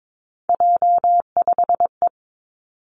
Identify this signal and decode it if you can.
Morse code, heard as a keyed tone, about 22 wpm, 700 Hz. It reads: J5E
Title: Three Challenging Letter-Number Combinations 22wpm